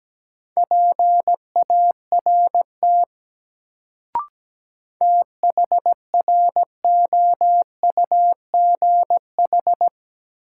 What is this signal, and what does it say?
Morse code, heard as a keyed tone, about 17 wpm, 700 Hz. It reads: PART THROUGH
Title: Sentences from Top 200 Words - New Word Drill - 17wpm